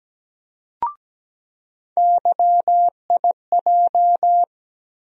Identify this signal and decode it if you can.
Morse code, heard as a keyed tone, about 17 wpm, 700 Hz. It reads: YIJ
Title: USA Call Sign Postfixes - 3 Letters 17wpm